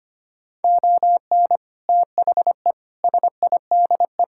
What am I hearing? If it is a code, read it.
Morse code, heard as a keyed tone, about 25 wpm, 700 Hz. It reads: ON THE SIDE
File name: practice / 25wpm / Sentences with 3-Word Phrases - 25wpm